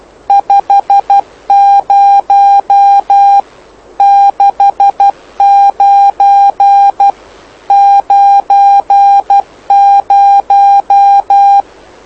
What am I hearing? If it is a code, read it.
Morse code, heard as a keyed tone, about 12 wpm, 800 Hz. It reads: 50 69 90
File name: lu4aao / lu7aa 09dic2011 exp planeador vuelo1